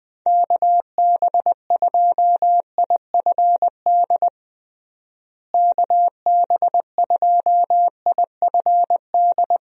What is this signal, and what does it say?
Morse code, heard as a keyed tone, about 21 wpm, 700 Hz. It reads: KB2IFD KB2IFD
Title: USA Call Signs - 6 Characters - Repeated 2x - 20wpm